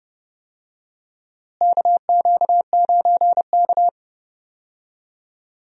Morse code, heard as a keyed tone, about 31 wpm, 700 Hz. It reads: KQ9K
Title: USA Call Signs - 4 Characters - Repeated 2x - 30wpm